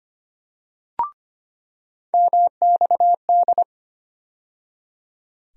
Morse code, heard as a keyed tone, about 25 wpm, 700 Hz. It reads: MXD
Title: USA Call Sign Postfixes - 3 Letters 25wpm